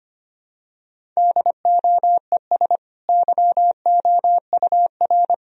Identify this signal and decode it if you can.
Morse code, heard as a keyed tone, about 25 wpm, 700 Hz. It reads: DOES YOUR
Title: Sentences from Top 1000 Words - Review Part 1 - 25wpm